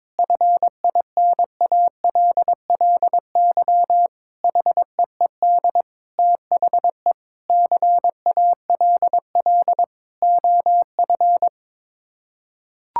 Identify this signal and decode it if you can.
Morse code, heard as a keyed tone, about 22 wpm, 700 Hz. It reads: FINALLY HEED THE CALL OF
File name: practice / 22wpm / Sets of 8 Words - 22wpm